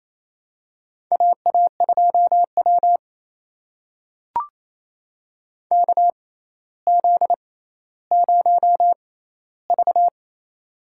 Morse code, heard as a keyed tone, about 28 wpm, spaced as for 14 wpm, 700 Hz. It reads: AA2W KZ0V
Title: USA Call Signs - 4 Characters - 6x Character Spacing - 28wpm